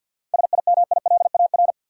Morse code, heard as a keyed tone, about 53 wpm, 700 Hz. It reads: SIMILAR
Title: Seven Letter Words - 50wpm